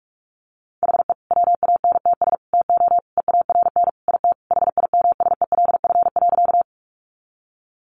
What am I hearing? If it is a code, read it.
Morse code, heard as a keyed tone, about 44 wpm, 700 Hz. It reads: HE WANTS TO EARN IT HIMSELF.
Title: Additional Easy Phrases 45wpm